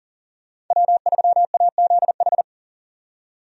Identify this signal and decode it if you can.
Morse code, heard as a keyed tone, about 40 wpm, 700 Hz. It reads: W3AZH
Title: USA Call Signs - 5 Characters - 40wpm